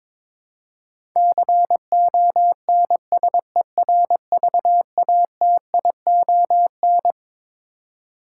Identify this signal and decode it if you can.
Morse code, heard as a keyed tone, about 21 wpm, 700 Hz. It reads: CONSERVATION
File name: Twelve Letter Words - 22wpm